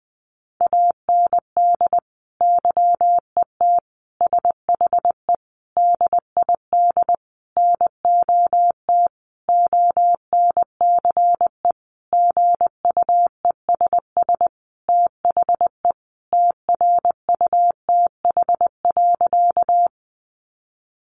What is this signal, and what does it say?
Morse code, heard as a keyed tone, about 20 wpm, 700 Hz. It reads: AND YET SHE DID NOT ONCE GUESS THE TRUTH.
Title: Sentences from Top 1500 Words 20wpm